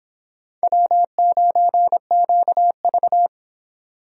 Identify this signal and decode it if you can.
Morse code, heard as a keyed tone, about 30 wpm, 700 Hz. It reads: W9QV